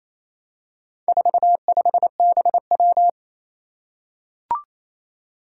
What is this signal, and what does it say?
Morse code, heard as a keyed tone, about 27 wpm, 700 Hz. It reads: 45BW
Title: Four Challenging Letter-Number Combinations 28wpm